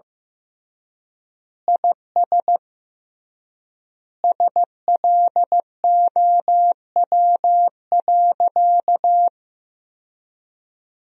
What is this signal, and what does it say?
Morse code, heard as a keyed tone, about 15 wpm, 700 Hz. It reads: IS SLOW.
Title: 3x Word-Spacing - Sentences from Top 400 Words - 15wpm